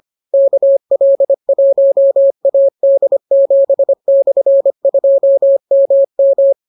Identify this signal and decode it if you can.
Morse code, heard as a keyed tone, about 25 wpm, 550 Hz. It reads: KL1AD7/2MM